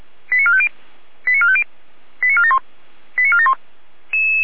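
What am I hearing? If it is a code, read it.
Five-tone selective call (ZVEI), heard as quick runs of five tones, tones of about 70 ms. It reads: 89450 89450 89471 89471
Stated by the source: ZVEI five-tone selective call received by SDR.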